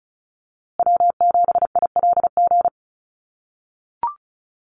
Morse code, heard as a keyed tone, about 34 wpm, 700 Hz. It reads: W7ILG
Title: USA Call Signs Repeated 3x 35wpm